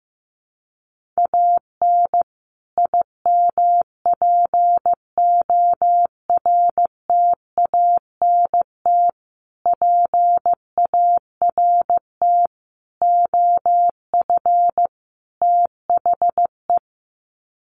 Morse code, heard as a keyed tone, about 15 wpm, 700 Hz. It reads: AN IMPORTANT PART OF THE
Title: Sets of 5 Words Encore 15wpm